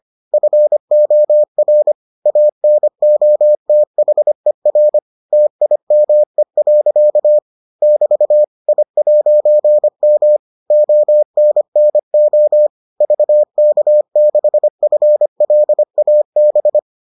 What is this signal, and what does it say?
Morse code, heard as a keyed tone, about 25 wpm, 600 Hz. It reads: FOR ANOTHER TIME. <BT> I'M ONNO VK6FLAB